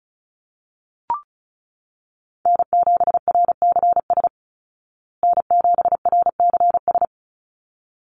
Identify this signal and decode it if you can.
Morse code, heard as a keyed tone, about 34 wpm, 700 Hz. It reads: N7RCS N7RCS
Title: Call Signs Repeated 3x - 35wpm